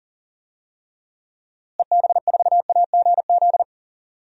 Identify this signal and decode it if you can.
Morse code, heard as a keyed tone, about 40 wpm, 700 Hz. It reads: EB4AGZ
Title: DXCC-100 Call Signs - 6 Characters - Repeated 3x - 40wpm